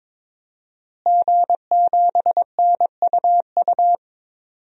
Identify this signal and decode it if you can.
Morse code, heard as a keyed tone, about 21 wpm, 700 Hz. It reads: G7NUU
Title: DXCC-100 Call Signs - 5 Characters - Repeated 2x - 22wpm